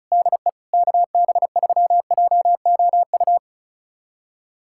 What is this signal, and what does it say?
Morse code, heard as a keyed tone, about 34 wpm, 700 Hz. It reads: DE KB3JOU
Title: USA Callsign DE Callsign - 6 Characters - 35wpm